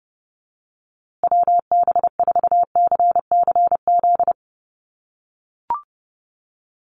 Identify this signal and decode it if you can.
Morse code, heard as a keyed tone, about 30 wpm, 700 Hz. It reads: WB4CCZ
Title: USA Call Signs Repeated 2x 30wpm